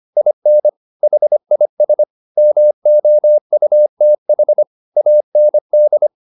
Morse code, heard as a keyed tone, about 25 wpm, 600 Hz. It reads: IN HIS MOUTH AND
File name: the old man and the sea part 1 25wpm